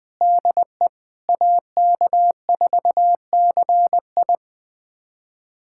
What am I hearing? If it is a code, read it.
Morse code, heard as a keyed tone, about 20 wpm, 700 Hz. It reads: DE AK4CI
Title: USA Callsign DE Callsign - 5 Characters - 20wpm